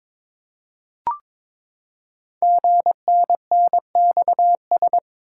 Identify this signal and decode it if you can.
Morse code, heard as a keyed tone, about 22 wpm, 700 Hz. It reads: GNNXS
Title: Code Groups - 5 Letters - 22wpm